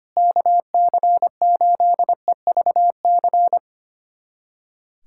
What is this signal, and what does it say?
Morse code, heard as a keyed tone, about 25 wpm, 700 Hz. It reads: KC8EVC
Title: USA Callsign DE Callsign 25wpm